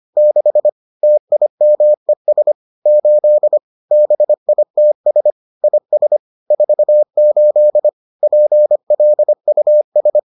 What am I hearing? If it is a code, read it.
Morse code, heard as a keyed tone, about 25 wpm, 600 Hz. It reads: 6 TIMES 8 BITS IS 48 PLUS